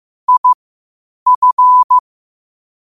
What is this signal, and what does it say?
Morse code, heard as a keyed tone, about 15 wpm, spaced as for 8 wpm, 1000 Hz. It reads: IF
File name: qotdfiles / 8wpm